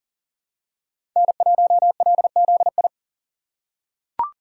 Morse code, heard as a keyed tone, about 40 wpm, 700 Hz. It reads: N1LZI
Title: USA Call Signs - 5 Characters - 6x Character Spacing - 40wpm